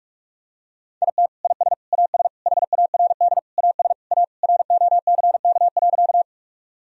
Morse code, heard as a keyed tone, about 41 wpm, 700 Hz. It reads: IT IS AS HARD AS A ROCK.
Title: Sentences from Top 1000 Words - Review Part 2 - 45wpm